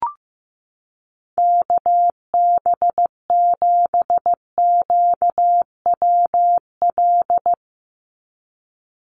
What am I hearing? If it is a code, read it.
Morse code, heard as a keyed tone, about 15 wpm, 700 Hz. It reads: KB7QWL
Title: USA Call Signs 15wpm